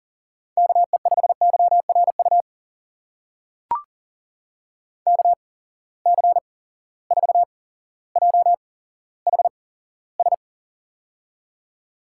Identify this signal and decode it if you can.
Morse code, heard as a keyed tone, about 40 wpm, spaced as for 17 wpm, 700 Hz. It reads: KE5YRU KC4JHS
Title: USA Call Signs - 6 Characters - 8x Character Spacing - 40wpm